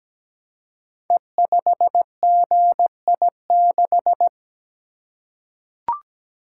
Morse code, heard as a keyed tone, about 17 wpm, 700 Hz. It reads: E5GI6
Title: Code Groups - 5 Letter:Numbers - 17wpm